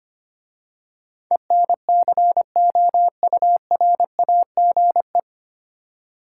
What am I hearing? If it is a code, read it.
Morse code, heard as a keyed tone, about 25 wpm, 690 Hz. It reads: ENCOURAGE
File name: practice / 25wpm / 9 Letter Words - First 3 Letters Given - 25wpm